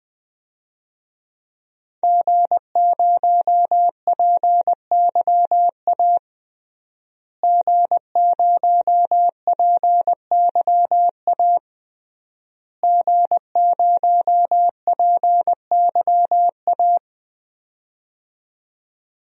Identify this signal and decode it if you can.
Morse code, heard as a keyed tone, about 20 wpm, 700 Hz. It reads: G0PYA G0PYA G0PYA
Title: DXCC-100 Call Signs - 5 Characters - Repeated 3x - 20wpm